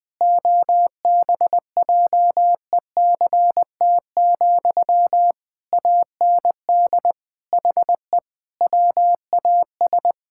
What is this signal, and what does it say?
Morse code, heard as a keyed tone, about 20 wpm, 700 Hz. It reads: OBJECT, AND HE WAS
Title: Sentences from Top 1000 Words - Review Part 6 - 20wpm